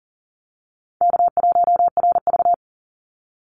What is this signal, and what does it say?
Morse code, heard as a keyed tone, about 40 wpm, 700 Hz. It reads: K1RV
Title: Easy Call Signs - 40wpm